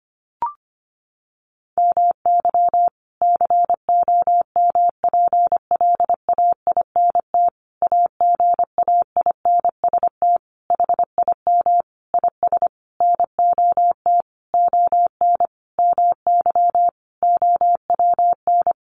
Morse code, heard as a keyed tone, about 25 wpm, 700 Hz. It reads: MY COMPLAINT AGAINST HIM IS NOT ON MY OWN
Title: Sentences from Top 1400 Words 25wpm